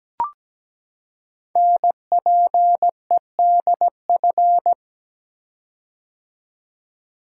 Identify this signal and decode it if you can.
Morse code, heard as a keyed tone, about 17 wpm, 700 Hz. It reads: NPEDF
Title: Code Groups - 5 Letter:Numbers - 17wpm